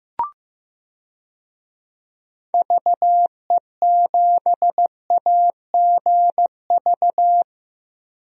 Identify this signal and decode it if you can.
Morse code, heard as a keyed tone, about 16 wpm, 700 Hz. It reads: VE7AGV
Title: DXCC-100 Call Signs - 6 Characters - Repeated 3x - 15wpm